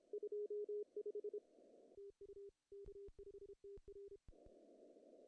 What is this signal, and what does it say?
Morse code, heard as a keyed tone, about 31 wpm, 410 Hz. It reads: 25 TU K5TR
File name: CW / cqww-cw-k5tr